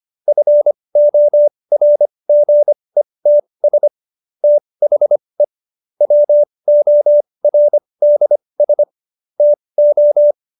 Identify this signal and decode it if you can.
Morse code, heard as a keyed tone, about 25 wpm, spaced as for 20 wpm, 590 Hz. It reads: FORGETS THE WORDS TO
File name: Jokes-25-char-20-eff